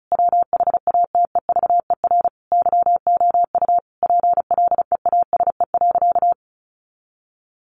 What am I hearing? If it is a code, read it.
Morse code, heard as a keyed tone, about 34 wpm, 700 Hz. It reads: WHATEVER YOU PLEASE.
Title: Sentences from Top 1500 Words 35wpm